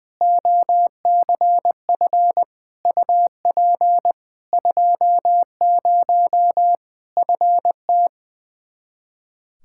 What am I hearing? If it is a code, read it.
Morse code, heard as a keyed tone, about 20 wpm, 700 Hz. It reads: OCF UP 20 FT
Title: Practice QSO - Exchange 2 - Info - 20wpm